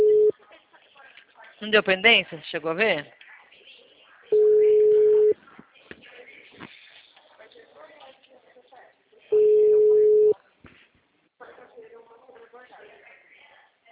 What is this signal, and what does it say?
Ringback tone (Europe, 425 Hz), with speech.
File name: out- -2007- - - .52717